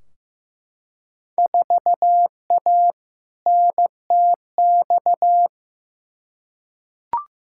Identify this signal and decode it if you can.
Morse code, heard as a keyed tone, about 15 wpm, 700 Hz. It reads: 4A NTX
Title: Field Day Exchanges - 15wpm